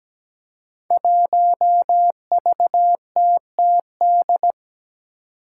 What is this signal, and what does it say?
Morse code, heard as a keyed tone, about 17 wpm, 700 Hz. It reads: 1VTTD
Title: Five Challenging Letter-Number Combinations 17wpm